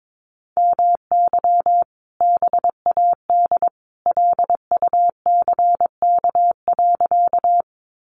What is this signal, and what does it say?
Morse code, heard as a keyed tone, about 22 wpm, 700 Hz. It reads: MY BAD LUCK.
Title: Sentences from Top 1500 Words 22wpm